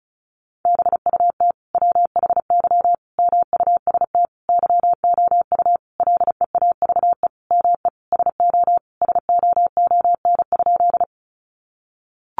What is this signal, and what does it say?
Morse code, heard as a keyed tone, about 34 wpm, 700 Hz. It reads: BUT WHY MUST YOU LEAVE ME SO SOON?
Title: Sentences from Top 400 Words 35wpm